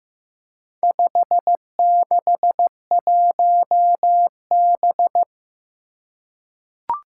Morse code, heard as a keyed tone, about 15 wpm, 700 Hz. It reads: 561B
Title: Four Challenging Letter-Number Combinations 15wpm